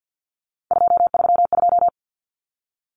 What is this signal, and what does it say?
Morse code, heard as a keyed tone, about 50 wpm, 700 Hz. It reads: 232
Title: Common Cut Numbers - 3-Digit 50wpm